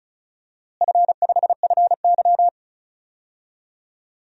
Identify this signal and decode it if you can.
Morse code, heard as a keyed tone, about 36 wpm, 700 Hz. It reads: F5FY
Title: DXCC-100 Call Signs - 4 Characters - Repeated 3x - 35wpm